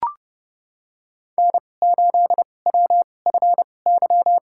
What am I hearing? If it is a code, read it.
Morse code, heard as a keyed tone, about 32 wpm, spaced as for 22 wpm, 700 Hz. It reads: N8WFY
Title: USA Call Signs - 5 Characters - 2x Character Spacing - 30wpm